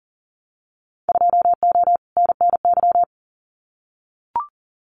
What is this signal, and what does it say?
Morse code, heard as a keyed tone, about 40 wpm, 700 Hz. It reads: 2O NNY
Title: Winter Field Day Exchanges 40wpm